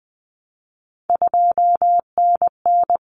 Morse code, heard as a keyed tone, about 20 wpm, 700 Hz. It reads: 2NN
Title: Common Cut Numbers - 3-Digit 20wpm